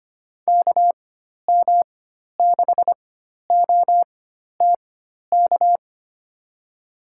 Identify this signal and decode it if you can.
Morse code, heard as a keyed tone, about 25 wpm, spaced as for 12 wpm, 700 Hz. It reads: KM6OTK
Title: USA Call Signs - 6 Characters - 4x Character Spacing - 25wpm